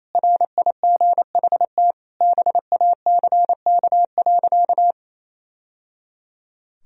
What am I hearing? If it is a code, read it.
Morse code, heard as a keyed tone, about 28 wpm, 700 Hz. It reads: RIGHT BACK.
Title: Sentences from Top 1000 Words - Review Part 4 - 28wpm